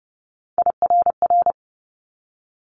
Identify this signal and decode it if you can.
Morse code, heard as a keyed tone, about 30 wpm, 700 Hz. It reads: IRR
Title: Multi-letter Phonemes 30wpm